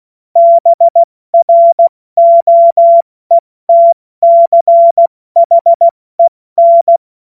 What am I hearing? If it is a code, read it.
Morse code, heard as a keyed tone, about 16 wpm, spaced as for 14 wpm, 680 Hz. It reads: BROETCHEN
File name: Morsekurs / Witze / Witz 27. Januar 2026 70 F80